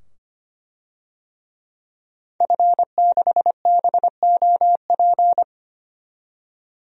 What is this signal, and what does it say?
Morse code, heard as a keyed tone, about 25 wpm, 700 Hz. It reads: F6BOP
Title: DXCC-100 Call Signs - 5 Characters - 25wpm